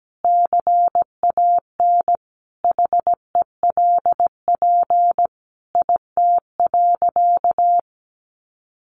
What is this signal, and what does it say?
Morse code, heard as a keyed tone, about 17 wpm, 700 Hz. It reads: CAN HELP IT.
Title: Sentences from Top 1100 Words 17wpm